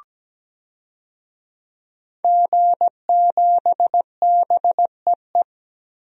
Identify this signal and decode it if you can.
Morse code, heard as a keyed tone, about 17 wpm, 700 Hz. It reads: G7BEE
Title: DXCC-100 Call Signs - 5 Characters - 17wpm